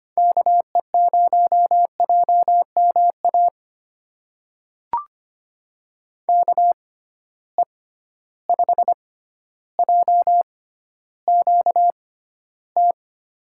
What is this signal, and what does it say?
Morse code, heard as a keyed tone, about 25 wpm, spaced as for 13 wpm, 700 Hz. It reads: KE0JMA KE5JQT
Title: USA Call Signs - 6 Characters - 6x Character Spacing - 25wpm